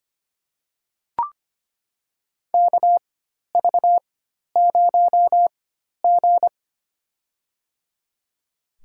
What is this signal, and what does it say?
Morse code, heard as a keyed tone, about 25 wpm, spaced as for 12 wpm, 700 Hz. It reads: KV0G
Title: USA Call Signs - 4 Characters - 4x Character Spacing - 25wpm